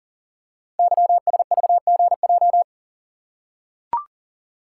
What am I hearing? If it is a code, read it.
Morse code, heard as a keyed tone, about 39 wpm, 700 Hz. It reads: YSVGJ
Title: Code Groups - 5 Letters - 40wpm